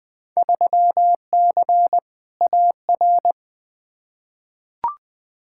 Morse code, heard as a keyed tone, about 20 wpm, 700 Hz. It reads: 3C AR
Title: Field Day Exchanges Encore - 20wpm